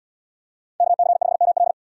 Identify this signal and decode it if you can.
Morse code, heard as a keyed tone, about 78 wpm, 700 Hz. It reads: KC5ML